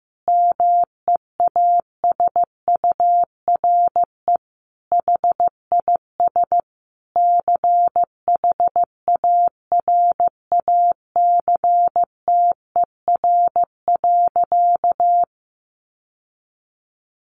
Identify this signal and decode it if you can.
Morse code, heard as a keyed tone, about 15 wpm, 700 Hz. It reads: MEASURE HIS CHARACTER.
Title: Sentences from Top 1200 Words 15wpm